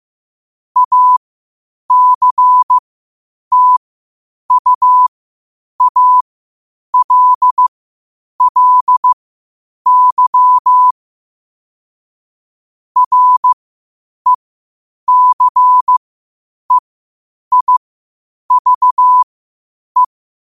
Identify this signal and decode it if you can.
Morse code, heard as a keyed tone, about 15 wpm, spaced as for 8 wpm, 1000 Hz. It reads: ACTUALLY RECEIVE
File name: qotdfiles / 8wpm